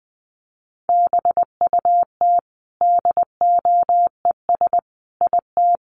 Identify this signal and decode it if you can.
Morse code, heard as a keyed tone, about 20 wpm, 700 Hz. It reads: BUT DOES IT
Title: Sentences from Top 1200 Words 20wpm